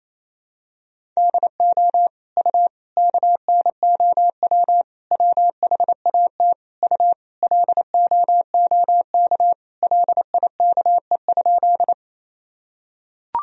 Morse code, heard as a keyed tone, about 27 wpm, 700 Hz. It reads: DO U KNOW WHAT U LOOK LIKE?
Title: Sentences from Top 100 Words and Abbreviations 28wpm